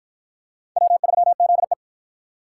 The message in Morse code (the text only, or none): W3ZE